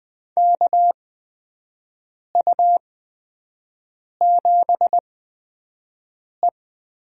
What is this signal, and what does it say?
Morse code, heard as a keyed tone, about 20 wpm, spaced as for 5 wpm, 700 Hz. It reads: KU7E
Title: USA Call Signs - 4 Characters - 8x Character Spacing - 20wpm